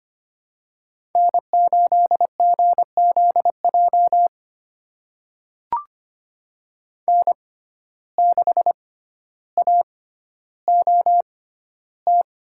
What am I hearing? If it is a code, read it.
Morse code, heard as a keyed tone, about 25 wpm, spaced as for 13 wpm, 700 Hz. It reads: N8GZJ N6AOT
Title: USA Call Signs - 5 Characters - 6x Character Spacing - 25wpm